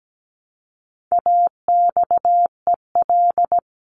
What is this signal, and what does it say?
Morse code, heard as a keyed tone, about 17 wpm, 700 Hz. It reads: AXEL
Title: CWT Exchange 17wpm